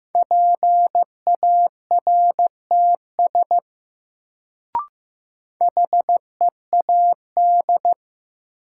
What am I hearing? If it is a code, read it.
Morse code, heard as a keyed tone, about 16 wpm, 700 Hz. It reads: PARTS HEAD
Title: Sets of 3 Words - New Word Drill - 15wpm